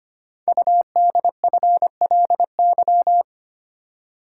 Morse code, heard as a keyed tone, about 25 wpm, 700 Hz. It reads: UDFLY
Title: Code Groups - 5 Letters - 25wpm